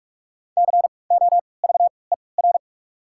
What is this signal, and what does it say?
Morse code, heard as a keyed tone, about 46 wpm, spaced as for 25 wpm, 700 Hz. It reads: COVER
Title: 3x Character Spacing - Top 500 Words - 45wpm